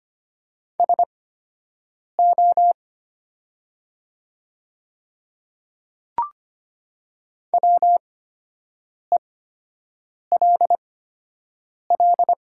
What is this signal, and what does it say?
Morse code, heard as a keyed tone, about 25 wpm, spaced as for 7 wpm, 700 Hz. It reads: SO WELL
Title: 8x Character Spacing - Top 200 Words - 25wpm